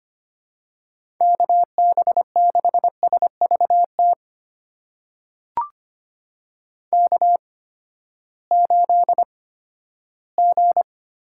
Morse code, heard as a keyed tone, about 25 wpm, spaced as for 14 wpm, 700 Hz. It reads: KB6SVT K8G
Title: USA Call Signs - 8x Character Spacing - 25wpm